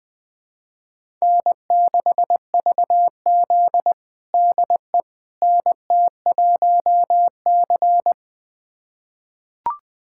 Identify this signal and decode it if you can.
Morse code, heard as a keyed tone, about 21 wpm, 700 Hz. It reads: N6VZ DE NT1C
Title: USA Callsign DE Callsign - 4 Characters - 20wpm